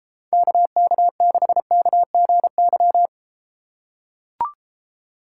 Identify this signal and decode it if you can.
Morse code, heard as a keyed tone, about 33 wpm, 700 Hz. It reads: KK6KGY